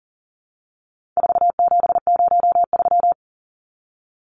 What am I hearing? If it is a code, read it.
Morse code, heard as a keyed tone, about 39 wpm, 700 Hz. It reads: 4703
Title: Numbers - 4-Digit 40wpm